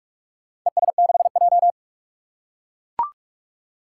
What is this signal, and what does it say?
Morse code, heard as a keyed tone, about 44 wpm, 700 Hz. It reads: ES6J
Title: Four Challenging Letter-Number Combinations 45wpm